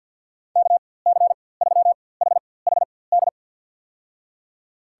Morse code, heard as a keyed tone, about 51 wpm, spaced as for 23 wpm, 700 Hz. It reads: KC3HHD
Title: USA Call Signs - 6 Characters - 4x Character Spacing - 50wpm